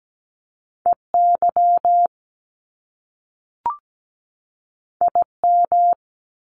Morse code, heard as a keyed tone, about 17 wpm, 700 Hz. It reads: EY IM
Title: Top Two Letter Combinations in Words 17wpm